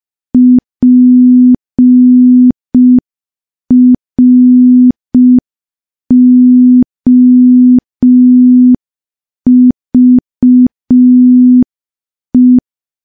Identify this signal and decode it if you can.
Morse code, heard as a keyed tone, about 5 wpm, 250 Hz. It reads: PROVE
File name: morse / morse-training-05WPM